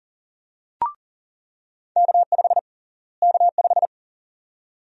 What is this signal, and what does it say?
Morse code, heard as a keyed tone, about 40 wpm, 700 Hz. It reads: K5 K5
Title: USA Call Sign Prefixes - Repeated 2x - 40wpm